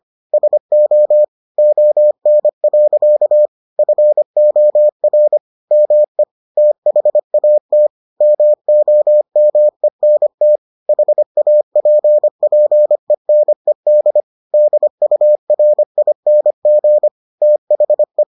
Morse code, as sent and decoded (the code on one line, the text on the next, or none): ... --- / --- -. .-.-.- / ..-. --- .-. / -- . / - .... .- - / -- --- -- . -. - / .... .- .--. .--. -. -.. / -.. ..- .-. .. -. --. / - .... .
SO ON. FOR ME THAT MOMENT HAPPENED DURING THE